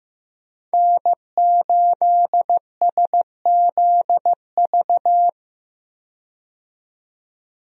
Morse code, heard as a keyed tone, about 15 wpm, 700 Hz. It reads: N8SZV
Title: USA Call Signs - 5 Characters - 15wpm